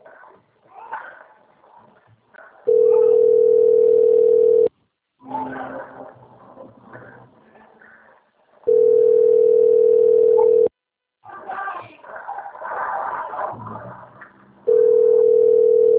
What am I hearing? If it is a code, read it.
Ringback tone (North America), 440 + 480 Hz.
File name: out- - - - - .60233